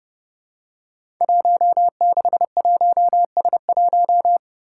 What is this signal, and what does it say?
Morse code, heard as a keyed tone, about 30 wpm, 700 Hz. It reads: 161S1
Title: Five Challenging Letter-Number Combinations 30wpm